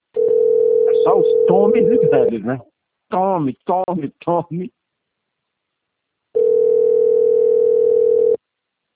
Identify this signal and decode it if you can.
Ringback tone (North America), 440 + 480 Hz, with speech.
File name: rg-1600-unknown- - - .4145